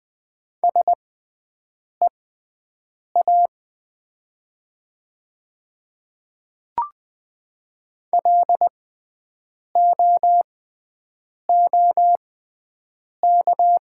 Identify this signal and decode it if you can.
Morse code, heard as a keyed tone, about 20 wpm, spaced as for 7 wpm, 700 Hz. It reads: SEA LOOK
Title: 6x Character Spacing - Top 300 Words - 20wpm